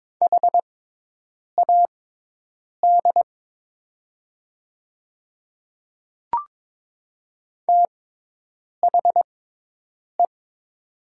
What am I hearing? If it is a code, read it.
Morse code, heard as a keyed tone, about 22 wpm, spaced as for 8 wpm, 700 Hz. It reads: HAD THE
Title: 6x Character Spacing - Top 100 Words - 22wpm